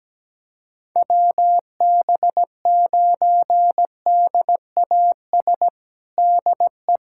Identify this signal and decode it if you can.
Morse code, heard as a keyed tone, about 17 wpm, 700 Hz. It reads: WB9DAS DE
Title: USA Callsign DE Callsign - 6 Characters - 17wpm